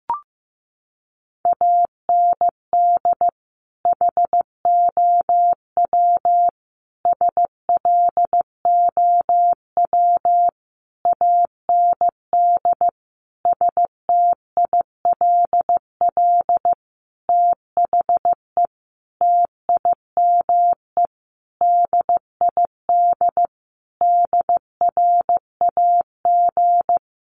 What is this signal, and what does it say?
Morse code, heard as a keyed tone, about 15 wpm, 700 Hz. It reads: AND HOW SLOW AND STILL THE TIME DID DRAG
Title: Sentences from Top 1500 Words 15wpm